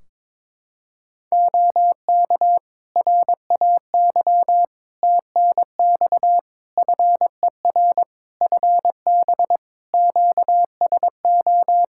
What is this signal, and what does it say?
Morse code, heard as a keyed tone, about 22 wpm, 700 Hz. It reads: OK RAY TNX FER FB QSO
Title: Practice QSO - Exchange 4 - Ending - 22wpm